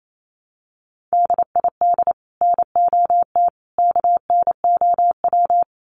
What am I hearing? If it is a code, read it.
Morse code, heard as a keyed tone, about 28 wpm, 700 Hz. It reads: DID NOT KNOW
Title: Sets of 3 Words 28wpm